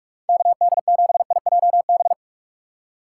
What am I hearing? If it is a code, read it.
Morse code, heard as a keyed tone, about 46 wpm, 700 Hz. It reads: KD7IJB